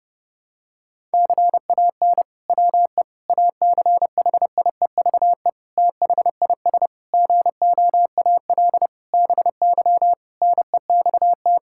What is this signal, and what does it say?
Morse code, heard as a keyed tone, about 30 wpm, 700 Hz. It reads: CAN WE ACHIEVE THIS GOAL BY NEXT
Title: Sentences from Top 1000 Words - Review Part 5 - 30wpm